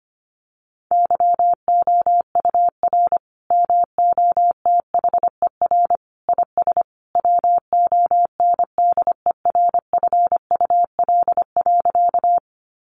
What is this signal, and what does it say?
Morse code, heard as a keyed tone, about 25 wpm, 700 Hz. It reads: YOUR MOTHER IS WONDERFUL.
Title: Sentences from Top 1100 Words 25wpm